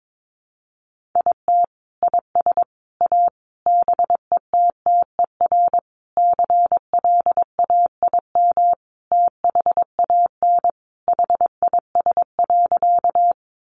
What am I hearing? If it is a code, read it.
Morse code, heard as a keyed tone, about 22 wpm, 700 Hz. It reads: IT IS A BETTER CLAIM THAN HIS.
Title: Sentences from Top 1500 Words 22wpm